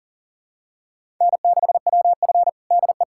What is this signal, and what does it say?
Morse code, heard as a keyed tone, about 40 wpm, 700 Hz. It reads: N6WF DE
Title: USA Callsign DE Callsign - 4 Characters - 40wpm